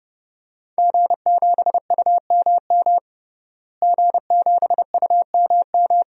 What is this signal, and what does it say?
Morse code, heard as a keyed tone, about 29 wpm, 700 Hz. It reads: G7UMM G7UMM
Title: DXCC-100 Call Signs - 5 Characters - Repeated 2x - 30wpm